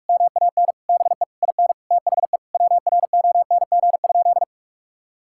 Morse code, heard as a keyed tone, about 46 wpm, 700 Hz. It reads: MAN BE IN THE WRONG?
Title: Sentences from Top 1000 Words - Review Part 6 - 45wpm